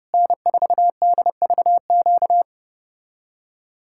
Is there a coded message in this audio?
Yes: Morse code from 0.1 to 2.4 s, heard as a keyed tone, about 32 wpm, 700 Hz. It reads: N4DVQ